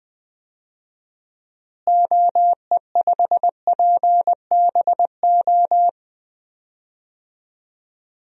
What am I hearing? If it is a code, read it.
Morse code, heard as a keyed tone, about 20 wpm, 700 Hz. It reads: OE5PBO
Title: DXCC-100 Call Signs - 6 Characters - Repeated 2x - 20wpm